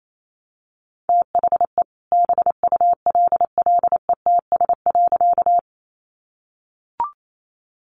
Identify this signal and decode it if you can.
Morse code, heard as a keyed tone, about 28 wpm, 700 Hz. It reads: THE BULLETS.
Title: A Sound Of Thunder 28wpm